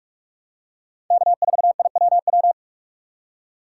Morse code, heard as a keyed tone, about 42 wpm, 690 Hz. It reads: K4IWW
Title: USA Call Signs - 5 Characters - 6x Character Spacing - 45wpm